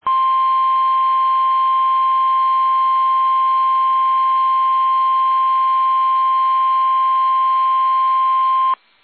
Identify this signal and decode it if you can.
Emergency Alert System (SAME) tones (the 1050 Hz attention tone). It is heard from 0.0 to 8.8 s.
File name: eas / KUSO / KUSO